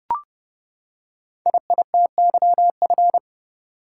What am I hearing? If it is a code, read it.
Morse code, heard as a keyed tone, about 30 wpm, 700 Hz. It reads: IITYF
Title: Code Groups - 5 Letters - 30wpm